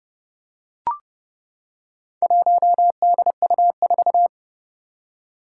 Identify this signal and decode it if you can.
Morse code, heard as a keyed tone, about 29 wpm, 690 Hz. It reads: 1DU4
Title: Four Challenging Letter-Number Combinations 30wpm